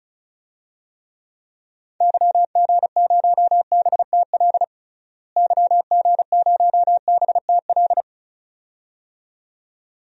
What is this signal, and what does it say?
Morse code, heard as a keyed tone, about 36 wpm, 700 Hz. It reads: YG0BTL YG0BTL
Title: DXCC-100 Call Signs - 6 Characters - Repeated 2x - 35wpm